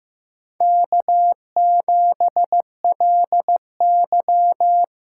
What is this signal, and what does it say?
Morse code, heard as a keyed tone, about 15 wpm, 700 Hz. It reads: K7LY
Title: USA Call Signs - 4 Characters - 4x Character Spacing - 15wpm